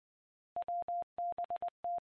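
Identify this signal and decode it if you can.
Morse code, heard as a keyed tone, about 20 wpm, 700 Hz. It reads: WBT
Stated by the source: Morse ident of the PAPA System analog repeaters, WD6FZA.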